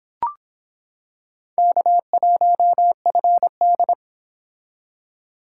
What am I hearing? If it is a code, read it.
Morse code, heard as a keyed tone, about 23 wpm, 700 Hz. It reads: K1FD